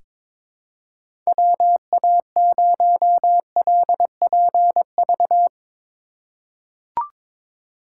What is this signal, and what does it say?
Morse code, heard as a keyed tone, about 22 wpm, 700 Hz. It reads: WA0LPV
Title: K1USN SST QSO Practice - 22wpm